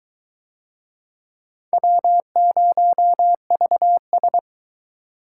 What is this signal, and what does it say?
Morse code, heard as a keyed tone, about 20 wpm, 700 Hz. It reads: W0VS